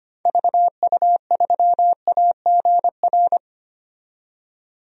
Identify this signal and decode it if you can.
Morse code, heard as a keyed tone, about 25 wpm, 690 Hz. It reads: VU3AGR
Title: DXCC-100 Call Signs - 6 Characters - 25wpm